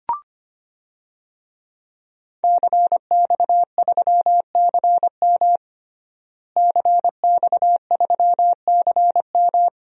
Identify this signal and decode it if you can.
Morse code, heard as a keyed tone, about 25 wpm, 700 Hz. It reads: CX3CM CX3CM
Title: DXCC-100 Call Signs - 5 Characters - Repeated 3x - 25wpm